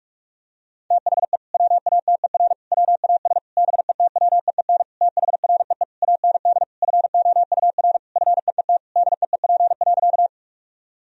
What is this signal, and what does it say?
Morse code, heard as a keyed tone, about 46 wpm, 700 Hz. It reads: THE WATER WAS BETWEEN THREE AND FOUR FEET DEEP.
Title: Sentences from Top 1000 Words - Review Part 3 - 45wpm